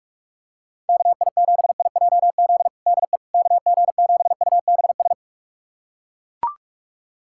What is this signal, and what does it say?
Morse code, heard as a keyed tone, about 41 wpm, 690 Hz. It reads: KI7IJZ DE KG7UBS
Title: USA Callsign DE Callsign 45wpm